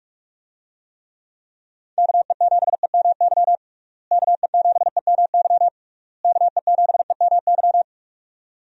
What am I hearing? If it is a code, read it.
Morse code, heard as a keyed tone, about 43 wpm, 700 Hz. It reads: KE7EMY KE7EMY KE7EMY
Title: USA Call Signs - 6 Characters - Repeated 3x - 45wpm